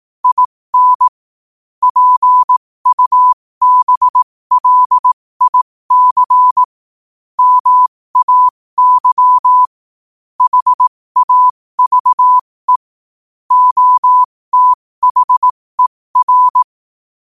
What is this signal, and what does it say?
Morse code, heard as a keyed tone, about 18 wpm, spaced as for 15 wpm, 1000 Hz. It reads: IN PUBLIC MAY HAVE OTHER
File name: qotdfiles / 15wpm